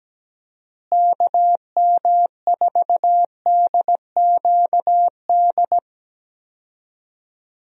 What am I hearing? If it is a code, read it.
Morse code, heard as a keyed tone, about 17 wpm, 700 Hz. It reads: KM4DQD
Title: USA Call Signs - 6 Characters - Repeated 3x - 17wpm